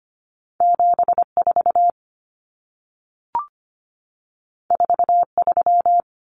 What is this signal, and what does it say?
Morse code, heard as a keyed tone, about 25 wpm, 700 Hz. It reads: 74 43
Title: Common Cut Numbers - 2-Digit 25wpm